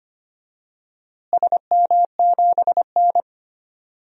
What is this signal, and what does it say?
Morse code, heard as a keyed tone, about 25 wpm, 700 Hz. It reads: SM7N
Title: DXCC-100 Call Signs - 4 Characters - Repeated 2x - 25wpm